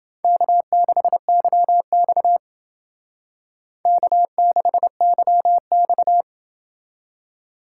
Morse code, heard as a keyed tone, about 28 wpm, 700 Hz. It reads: K6YX K6YX